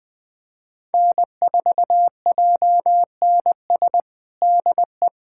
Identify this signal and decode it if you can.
Morse code, heard as a keyed tone, about 20 wpm, 700 Hz. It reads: N4JNS DE
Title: USA Callsign DE Callsign 20wpm